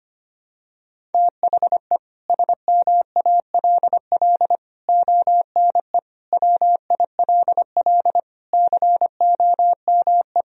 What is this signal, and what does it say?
Morse code, heard as a keyed tone, about 25 wpm, 700 Hz. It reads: THE SMALL ONE WILL COME
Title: Sentences from Top 1000 Words - Review Part 1 - 25wpm